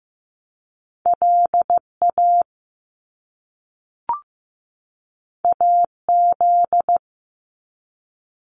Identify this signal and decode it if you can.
Morse code, heard as a keyed tone, about 15 wpm, 700 Hz. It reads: LA AZ
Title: US States 15wpm